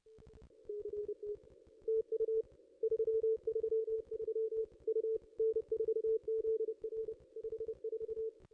Morse code, heard as a keyed tone, about 29 wpm, 430 Hz. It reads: CT TU 333 U N4ZR 54
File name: contests / ss-cw-k5tr-op-n5rz -0100